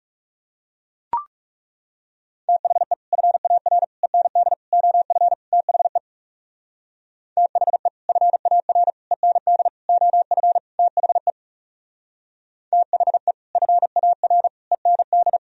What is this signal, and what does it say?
Morse code, heard as a keyed tone, about 37 wpm, 700 Hz. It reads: THE FAR END OF THE THE FAR END OF THE THE FAR END
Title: Speed-Racing - Sets of 5 Words 30wpm